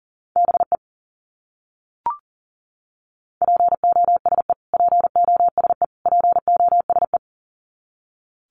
Morse code, heard as a keyed tone, about 40 wpm, 700 Hz. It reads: BE POSE POSE POSE
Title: Top 500 Words Repeated 40wpm